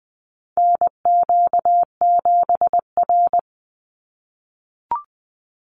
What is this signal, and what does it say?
Morse code, heard as a keyed tone, about 20 wpm, 700 Hz. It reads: NQ7R
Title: NA SOTA Chaser Callsigns 20wpm